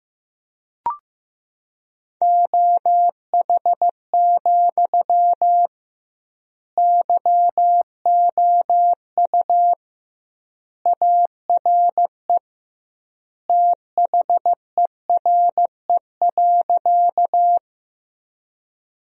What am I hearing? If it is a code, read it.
Morse code, heard as a keyed tone, about 16 wpm, 700 Hz. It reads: OH, YOU ARE THERE.
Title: 2x Word-Spacing - Sentences from Top 500 Words - 15wpm